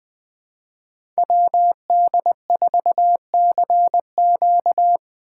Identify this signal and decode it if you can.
Morse code, heard as a keyed tone, about 23 wpm, 700 Hz. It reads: WD4CQ
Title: Speed Racing - USA 2x2 Call Signs 20wpm